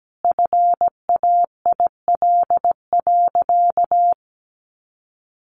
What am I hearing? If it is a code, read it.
Morse code, heard as a keyed tone, about 17 wpm, 700 Hz. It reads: FAIL.
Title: Sentences from Top 1500 Words 17wpm